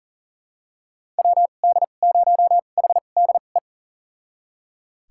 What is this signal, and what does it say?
Morse code, heard as a keyed tone, about 40 wpm, spaced as for 29 wpm, 700 Hz. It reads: WD0HDE
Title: USA Call Signs - 6 Characters - 2x Character Spacing - 40wpm